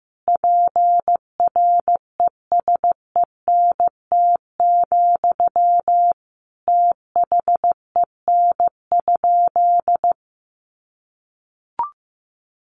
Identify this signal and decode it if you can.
Morse code, heard as a keyed tone, about 15 wpm, 700 Hz. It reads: PRESENT, THEN?
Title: Sentences from Top 800 Words 15wpm